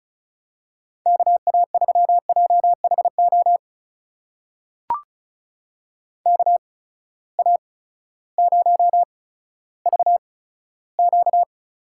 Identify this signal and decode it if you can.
Morse code, heard as a keyed tone, about 36 wpm, spaced as for 16 wpm, 700 Hz. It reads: KA3JHO KA0VQ
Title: USA Call Signs - 6 Characters - 8x Character Spacing - 35wpm